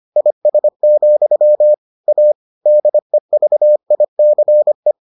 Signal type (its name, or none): Morse code